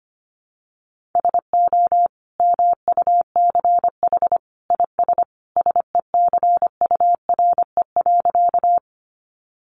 Morse code, heard as a keyed tone, about 25 wpm, 700 Hz. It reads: SO MUCH IS SECURE.
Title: Sentences from Top 1200 Words 25wpm